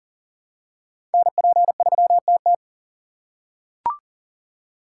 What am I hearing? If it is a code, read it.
Morse code, heard as a keyed tone, about 40 wpm, 700 Hz. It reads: NP3TT
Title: USA Call Signs - 4x Character Spacing - 40wpm